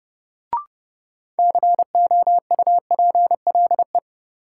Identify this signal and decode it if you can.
Morse code, heard as a keyed tone, about 30 wpm, 700 Hz. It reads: COUPLE
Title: Sets of 2 Words - New Word Drill - 30wpm